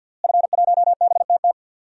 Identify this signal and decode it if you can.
Morse code, heard as a keyed tone, about 50 wpm, 700 Hz. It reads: F1BTT
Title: DXCC-100 Call Signs - 5 Characters - Repeated 3x - 50wpm